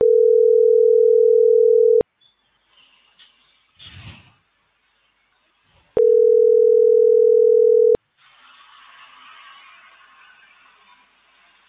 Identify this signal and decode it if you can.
Ringback tone (North America), 440 + 480 Hz.